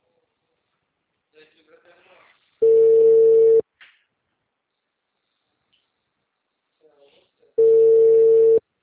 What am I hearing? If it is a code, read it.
Ringback tone (China, 450 Hz), with speech.
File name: out- -2014- - - .14935